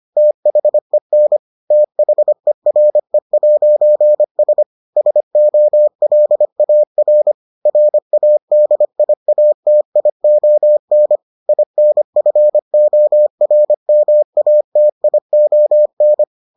Morse code, heard as a keyed tone, about 25 wpm, 600 Hz. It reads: THEN THERE'S SOLAR RADIATION INFORMATION